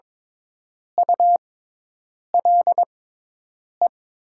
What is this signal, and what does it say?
Morse code, heard as a keyed tone, about 22 wpm, spaced as for 8 wpm, 700 Hz. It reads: ULE
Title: 6x Character Spacing - Top 500 Words - 22wpm